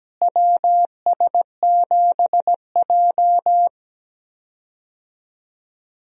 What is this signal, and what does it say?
Morse code, heard as a keyed tone, about 17 wpm, 700 Hz. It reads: WS7J
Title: USA Call Signs - 4 Characters - Repeated 3x - 17wpm